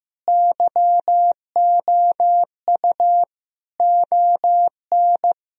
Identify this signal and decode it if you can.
Morse code, heard as a keyed tone, about 15 wpm, 700 Hz. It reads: YOU ON
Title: Sentences from Top 1000 Words - Review Part 1 - 15wpm